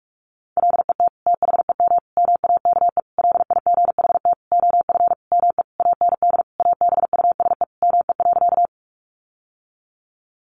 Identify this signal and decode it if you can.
Morse code, heard as a keyed tone, about 44 wpm, 700 Hz. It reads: LET THEM MAKE LIGHT OF ME AND ABUSE ME.
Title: Sentences from Top 1300 Words 45wpm